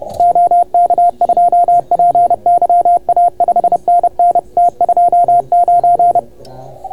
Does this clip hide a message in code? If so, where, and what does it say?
Yes: Morse code from 0.2 to 6.2 s, heard as a keyed tone, about 31 wpm, 690 Hz. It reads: OK2PYA 5NN T29